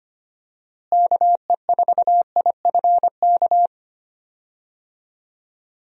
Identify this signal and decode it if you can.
Morse code, heard as a keyed tone, about 25 wpm, 700 Hz. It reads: KE4IFK
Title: USA Call Signs - 6 Characters - Repeated 3x - 25wpm